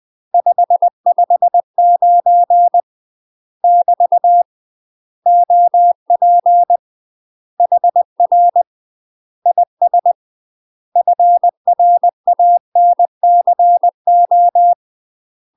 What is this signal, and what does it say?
Morse code, heard as a keyed tone, about 20 wpm, 700 Hz. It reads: 559 <BT> OP HR IS FRANCO
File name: UK MASTER QSOs TRAINING / 20WPM / UK MASTER QSOs 0063